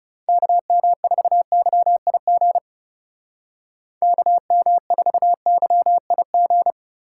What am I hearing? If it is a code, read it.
Morse code, heard as a keyed tone, about 40 wpm, 700 Hz. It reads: KM4YIG KM4YIG